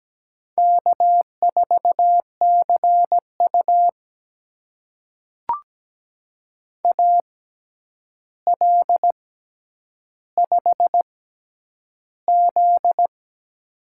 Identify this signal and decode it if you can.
Morse code, heard as a keyed tone, about 17 wpm, spaced as for 9 wpm, 700 Hz. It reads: K4CU AL5Z
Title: USA Call Signs - 4 Characters - 6x Character Spacing - 17wpm